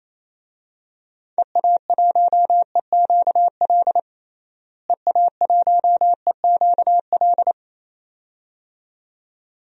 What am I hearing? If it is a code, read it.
Morse code, heard as a keyed tone, about 27 wpm, 700 Hz. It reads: EA1EQL EA1EQL
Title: DXCC-100 Call Signs - 6 Characters - Repeated 2x - 28wpm